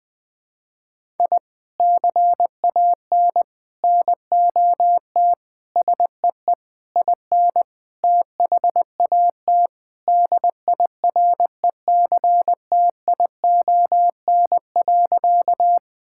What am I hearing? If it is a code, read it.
Morse code, heard as a keyed tone, about 21 wpm, 700 Hz. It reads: I CAN NOT SEE IN THAT DIRECTION.
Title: Sentences from Top 1000 Words - Review Part 5 - 20wpm